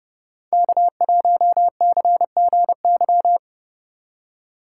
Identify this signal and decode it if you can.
Morse code, heard as a keyed tone, about 29 wpm, 700 Hz. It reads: K1CGY